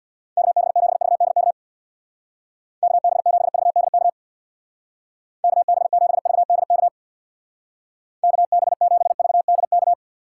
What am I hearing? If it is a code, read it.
Morse code, heard as a keyed tone, about 64 wpm, 700 Hz. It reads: KB7VDK KB7VDK KB7VDK KB7VDK